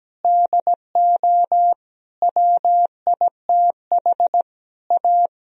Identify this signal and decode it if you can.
Morse code, heard as a keyed tone, about 17 wpm, 700 Hz. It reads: DO WITH A
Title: Sentences from Top 1000 Words - Review Part 2 - 17wpm